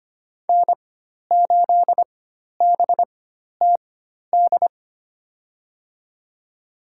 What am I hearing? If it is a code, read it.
Morse code, heard as a keyed tone, about 25 wpm, spaced as for 12 wpm, 700 Hz. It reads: N8BTD
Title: USA Call Signs - 5 Characters - 4x Character Spacing - 25wpm